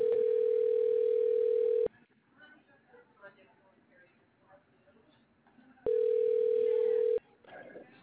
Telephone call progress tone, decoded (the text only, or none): ringback tone